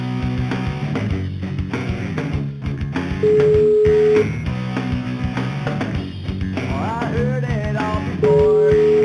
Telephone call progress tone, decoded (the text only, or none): ringback tone